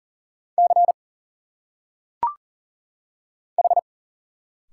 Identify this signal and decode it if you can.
Morse code, heard as a keyed tone, about 40 wpm, 700 Hz. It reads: C H
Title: Single Letters - Phonetic Pronunciation 40wpm